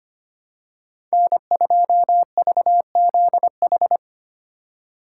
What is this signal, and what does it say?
Morse code, heard as a keyed tone, about 25 wpm, 700 Hz. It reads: N2VZH
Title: USA Call Signs - 5 Characters - 4x Character Spacing - 25wpm